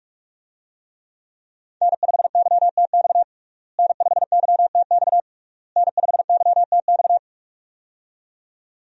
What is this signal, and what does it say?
Morse code, heard as a keyed tone, about 41 wpm, 700 Hz. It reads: N5YTX N5YTX N5YTX
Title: USA Call Signs - 5 Characters - Repeated 3x - 45wpm